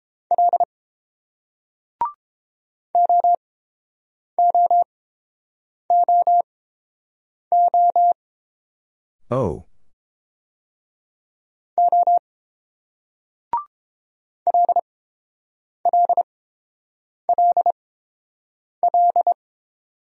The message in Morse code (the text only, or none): L OOOO O LLLL